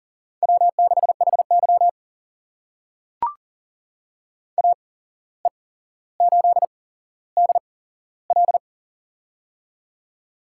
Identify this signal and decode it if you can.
Morse code, heard as a keyed tone, about 40 wpm, spaced as for 16 wpm, 700 Hz. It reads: W6HY AE8DL
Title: USA Call Signs - 8x Character Spacing - 40wpm